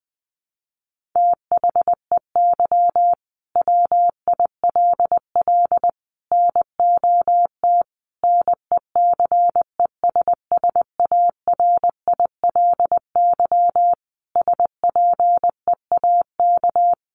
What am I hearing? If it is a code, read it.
Morse code, heard as a keyed tone, about 20 wpm, 700 Hz. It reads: THEY WILL NOT NECESSARILY SPEAK
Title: Sentences from Top 1400 Words 20wpm